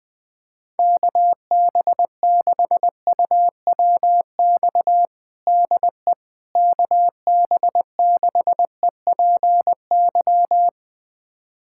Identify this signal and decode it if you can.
Morse code, heard as a keyed tone, about 20 wpm, 700 Hz. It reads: KB6UWX DE KB6EPY
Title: USA Callsign DE Callsign 20wpm